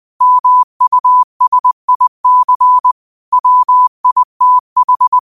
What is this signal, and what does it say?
Morse code, heard as a keyed tone, about 20 wpm, 1000 Hz. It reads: MUSIC WITH